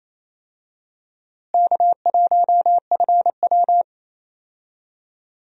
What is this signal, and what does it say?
Morse code, heard as a keyed tone, about 28 wpm, 700 Hz. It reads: K1FW
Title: USA Call Signs - 4 Characters - Repeated 3x - 28wpm